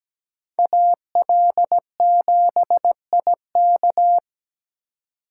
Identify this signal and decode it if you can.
Morse code, heard as a keyed tone, about 20 wpm, 700 Hz. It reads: AL7IK
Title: Speed Racing - USA 2x2 Call Signs 17wpm